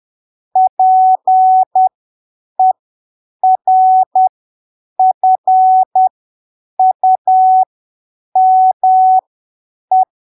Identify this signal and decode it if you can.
Morse code, heard as a keyed tone, about 10 wpm, spaced as for 7 wpm, 750 Hz. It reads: PERFUME
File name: arquivos / texto / texto 20 10ppm